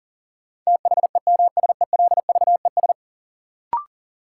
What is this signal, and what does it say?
Morse code, heard as a keyed tone, about 40 wpm, 700 Hz. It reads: THEMSELVES
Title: Sentences from Top 800 Words - New Word Drill - 40wpm